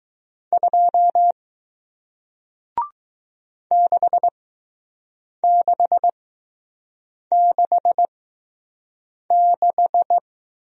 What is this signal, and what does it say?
Morse code, heard as a keyed tone, about 20 wpm, spaced as for 6 wpm, 700 Hz. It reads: 2 6666